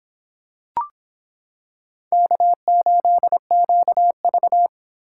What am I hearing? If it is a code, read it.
Morse code, heard as a keyed tone, about 26 wpm, 700 Hz. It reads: K8QV